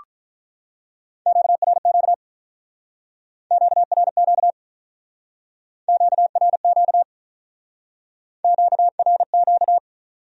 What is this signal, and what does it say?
Morse code, heard as a keyed tone, about 45 wpm, 700 Hz. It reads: QRQ QRQ QRQ QRQ